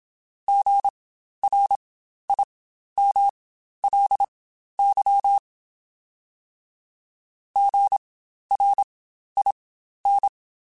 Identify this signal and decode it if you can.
Morse code, heard as a keyed tone, about 27 wpm, spaced as for 12 wpm, 800 Hz. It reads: GRIMLY GRIN
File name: Spaced Alphas / Spaced Alphas mp3s / Gs sp @25